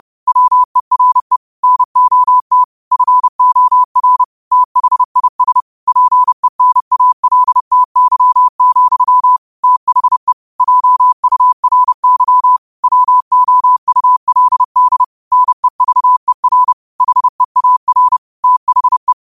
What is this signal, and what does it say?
Morse code, heard as a keyed tone, about 30 wpm, 1000 Hz. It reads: WERE NOT FOR THIS PENALTY, THE JURY WOULD NEVER HEAR THE